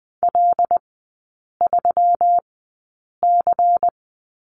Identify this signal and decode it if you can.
Morse code, heard as a keyed tone, about 20 wpm, spaced as for 8 wpm, 700 Hz. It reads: L3C
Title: 166 Lesson 27e - Number 7 - Three Characters 20wpm